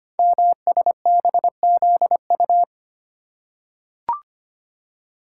Morse code, heard as a keyed tone, about 25 wpm, 700 Hz. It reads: MSBZU
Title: Code Groups - 5 Letters - 25wpm